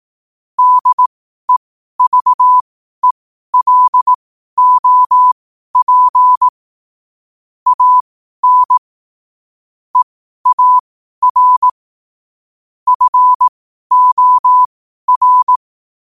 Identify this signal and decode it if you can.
Morse code, heard as a keyed tone, about 18 wpm, spaced as for 12 wpm, 1000 Hz. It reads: DEVELOP AN EAR FOR
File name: qotdfiles / 12wpm